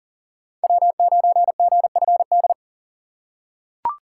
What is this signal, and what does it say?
Morse code, heard as a keyed tone, about 40 wpm, 700 Hz. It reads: W9GFD
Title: USA Call Signs - 5 Characters - 2x Character Spacing - 40wpm